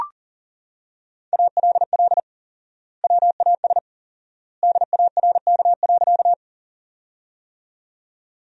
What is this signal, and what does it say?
Morse code, heard as a keyed tone, about 40 wpm, 700 Hz. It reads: ALL WAS DARK.
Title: 4x Word-Spacing - Sentences from Top 500 Words - 40wpm